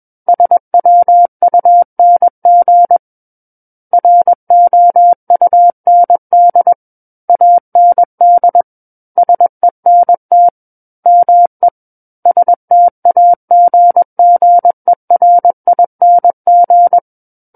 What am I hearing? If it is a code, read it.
Morse code, heard as a keyed tone, about 21 wpm, 700 Hz. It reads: SWUNG ROUND AND SENT ME STAGGERING
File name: Book One / 12-Chapter 12 - 21 WPM